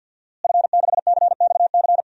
Morse code, heard as a keyed tone, about 52 wpm, 700 Hz. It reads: F6CXC
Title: DXCC-100 Call Signs - 5 Characters - Repeated 2x - 50wpm